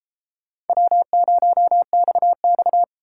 Morse code, heard as a keyed tone, about 32 wpm, 700 Hz. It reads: W0XX